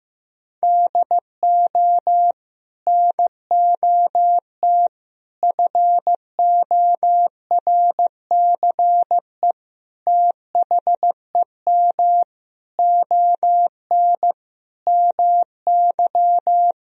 Morse code, heard as a keyed tone, about 15 wpm, 700 Hz. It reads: DO NOT FORCE THEM ON MY
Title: Sentences from Top 1000 Words - Review Part 3 - 15wpm